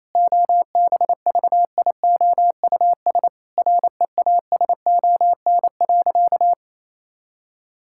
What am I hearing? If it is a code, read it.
Morse code, heard as a keyed tone, about 28 wpm, 700 Hz. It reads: OBVIOUS REASON.
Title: Sentences from Top 1600 Words 28wpm